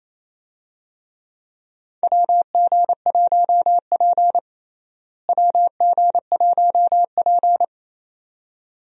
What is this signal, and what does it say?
Morse code, heard as a keyed tone, about 28 wpm, 700 Hz. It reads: WG1P WG1P
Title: USA Call Signs - 4 Characters - Repeated 2x - 28wpm